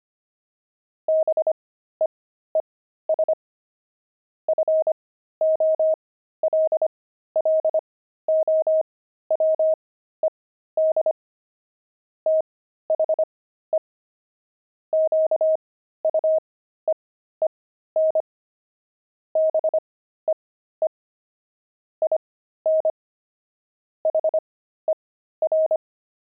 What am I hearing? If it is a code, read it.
Morse code, heard as a keyed tone, about 25 wpm, spaced as for 13 wpm, 630 Hz. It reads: BEES FOLLOWED THE QUEEN BEE IN HER
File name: pangram-fs-13-cs-25-beehive